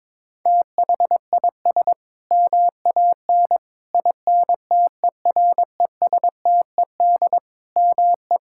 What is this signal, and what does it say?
Morse code, heard as a keyed tone, about 22 wpm, 700 Hz. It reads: THIS MAN INTERESTED ME
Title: Sentences from Top 1000 Words - Review Part 6 - 22wpm